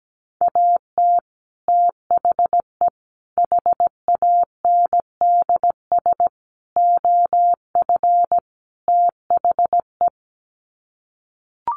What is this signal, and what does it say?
Morse code, heard as a keyed tone, about 17 wpm, 700 Hz. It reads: AT THE HANDS OF THE
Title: Sets of 5 Words Encore2 17wpm